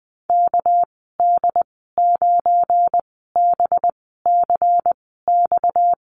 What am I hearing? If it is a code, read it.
Morse code, heard as a keyed tone, about 20 wpm, spaced as for 14 wpm, 700 Hz. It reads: KD9BCX
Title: 238 Lesson 38g - Letter q - Call Signs 20wpm